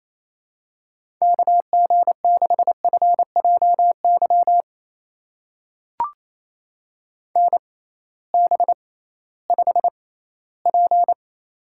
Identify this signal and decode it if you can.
Morse code, heard as a keyed tone, about 28 wpm, spaced as for 16 wpm, 700 Hz. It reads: KG6FJY NB5P
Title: USA Call Signs - 6x Character Spacing - 28wpm